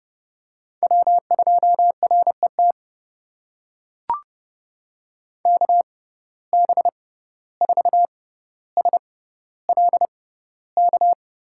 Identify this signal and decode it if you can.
Morse code, heard as a keyed tone, about 30 wpm, spaced as for 15 wpm, 700 Hz. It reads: W2RET KB4SLK
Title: USA Call Signs - 6x Character Spacing - 30wpm